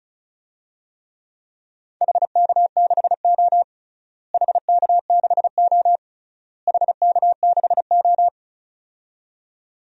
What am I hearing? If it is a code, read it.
Morse code, heard as a keyed tone, about 34 wpm, 700 Hz. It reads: HK6O HK6O HK6O
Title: DXCC-100 Call Signs - 4 Characters - Repeated 3x - 35wpm